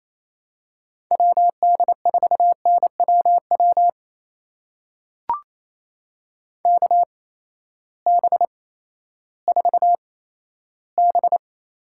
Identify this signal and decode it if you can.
Morse code, heard as a keyed tone, about 28 wpm, spaced as for 14 wpm, 700 Hz. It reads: WD4NWW KB4B
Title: USA Call Signs - 6 Characters - 8x Character Spacing - 28wpm